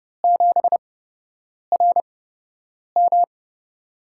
Morse code, heard as a keyed tone, about 32 wpm, spaced as for 8 wpm, 700 Hz. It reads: 7RM
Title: USA Call Signs - 8x Character Spacing - 30wpm